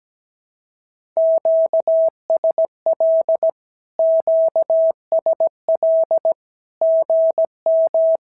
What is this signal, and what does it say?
Morse code, heard as a keyed tone, about 17 wpm, 650 Hz. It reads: QSL QSL GM
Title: SOTA QSO Practice 17wpm